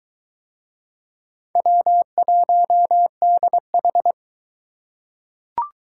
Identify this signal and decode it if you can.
Morse code, heard as a keyed tone, about 22 wpm, 700 Hz. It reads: W1DH